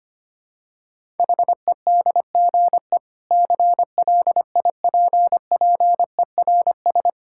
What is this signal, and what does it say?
Morse code, heard as a keyed tone, about 25 wpm, 700 Hz. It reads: HEDGE CLIPPERS
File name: practice / 25wpm / Difficult Words - Speed-Racing with Hints - 25wpm